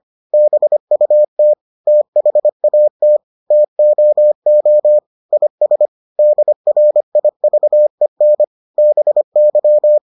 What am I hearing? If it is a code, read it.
Morse code, heard as a keyed tone, about 25 wpm, 600 Hz. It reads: BUT THAT TOO IS DRIVEN BY